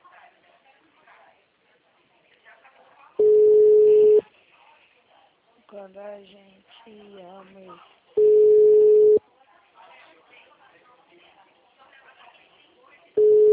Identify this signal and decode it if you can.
Ringback tone (Europe, 425 Hz), with speech.
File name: out- -2023- - - .33152